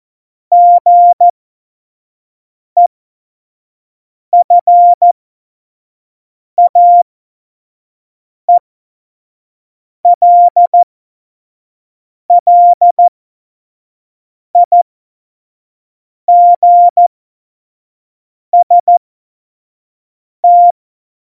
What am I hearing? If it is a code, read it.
Morse code, heard as a keyed tone, about 14 wpm, spaced as for 5 wpm, 700 Hz. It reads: GEFAELLIGST
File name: Morsekurs / Witze / Witz 21. November 2017, 25 F70